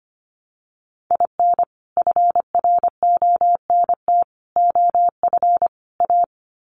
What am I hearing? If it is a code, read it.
Morse code, heard as a keyed tone, about 25 wpm, 700 Hz. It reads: IN FRONT OF A
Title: Sets of 4 Words 25wpm